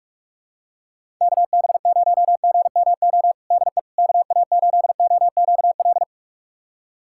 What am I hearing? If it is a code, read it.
Morse code, heard as a keyed tone, about 46 wpm, 700 Hz. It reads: KB0GMO DE KA8OQL
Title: USA Callsign DE Callsign 45wpm